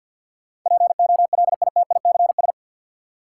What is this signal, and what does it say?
Morse code, heard as a keyed tone, about 52 wpm, 700 Hz. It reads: POLITICS
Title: Sentences from Top 1000 Words - New Word Drill - 50wpm